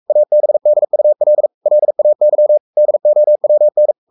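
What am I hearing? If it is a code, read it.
Morse code, heard as a keyed tone, about 44 wpm, 600 Hz. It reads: ABDUL LAY DOWN
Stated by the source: Morse code at 43 wpm sending Ken Follett's novel Never in English, from CW med Gnister.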